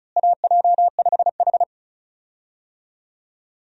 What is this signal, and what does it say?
Morse code, heard as a keyed tone, about 36 wpm, 700 Hz. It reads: AJ5H
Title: USA Call Signs - 4 Characters - Repeated 2x - 35wpm